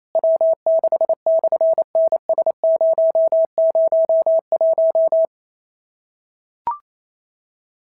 Morse code, heard as a keyed tone, about 28 wpm, 650 Hz. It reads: W6/NS001
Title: SOTA QSO Practice 28wpm